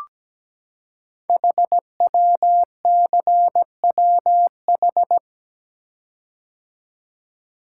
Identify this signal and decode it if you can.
Morse code, heard as a keyed tone, about 17 wpm, 700 Hz. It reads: HWCWH
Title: Code Groups - 5 Letters - 17wpm